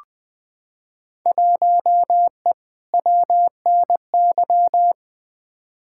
Morse code, heard as a keyed tone, about 20 wpm, 700 Hz. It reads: 1E WNY
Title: Field Day Exchanges - 20wpm